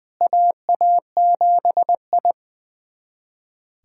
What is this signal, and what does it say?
Morse code, heard as a keyed tone, about 20 wpm, 700 Hz. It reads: AA7I
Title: USA Callsign DE Callsign - 4 Characters - 20wpm